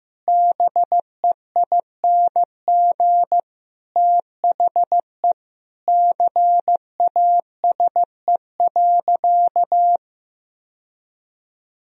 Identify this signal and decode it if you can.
Morse code, heard as a keyed tone, about 15 wpm, 700 Hz. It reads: BEING THE CASE.
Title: Sentences from Top 1000 Words - Review Part 3 - 15wpm